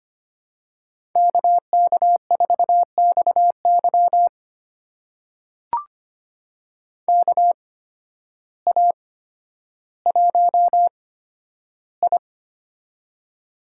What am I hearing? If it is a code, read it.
Morse code, heard as a keyed tone, about 25 wpm, spaced as for 12 wpm, 700 Hz. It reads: KK4XY KA1I
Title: USA Call Signs - 8x Character Spacing - 25wpm